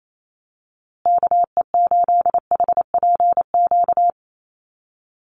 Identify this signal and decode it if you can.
Morse code, heard as a keyed tone, about 28 wpm, 700 Hz. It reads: KE8HPQ
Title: USA Call Signs Repeated 3x 28wpm